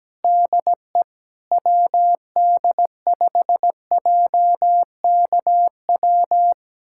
Morse code, heard as a keyed tone, about 17 wpm, 700 Hz. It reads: DE WD5JKW
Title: USA Callsign DE Callsign - 6 Characters - 17wpm